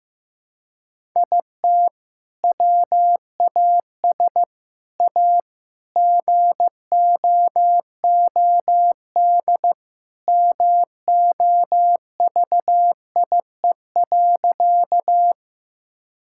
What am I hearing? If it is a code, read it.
Morse code, heard as a keyed tone, about 15 wpm, 700 Hz. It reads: IT WAS A GOOD MOVIE.
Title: Sentences from Top 1000 Words - Review Part 5 - 15wpm